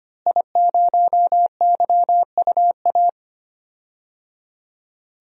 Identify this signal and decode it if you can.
Morse code, heard as a keyed tone, about 25 wpm, 700 Hz. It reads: I0YUA
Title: DXCC-100 Call Signs - 5 Characters - Repeated 3x - 25wpm